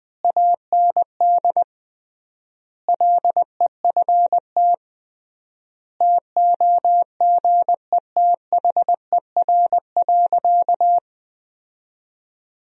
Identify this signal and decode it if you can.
Morse code, heard as a keyed tone, about 20 wpm, 700 Hz. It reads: AND LEFT TOGETHER.
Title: 3x Word-Spacing - Sentences from Top 400 Words - 20wpm